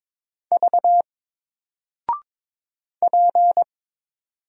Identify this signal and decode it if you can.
Morse code, heard as a keyed tone, about 22 wpm, 700 Hz. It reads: V P
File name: Single-Letter-Number-22wpm